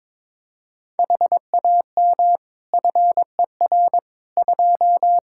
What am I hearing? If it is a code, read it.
Morse code, heard as a keyed tone, about 21 wpm, 700 Hz. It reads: HAM FER 2
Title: Practice QSO - Exchange 3 - More Info - 22wpm